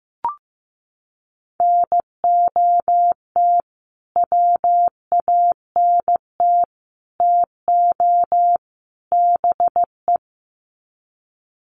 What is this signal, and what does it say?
Morse code, heard as a keyed tone, about 15 wpm, 700 Hz. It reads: NOT WANT TO BE
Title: Sets of 4 Words 15wpm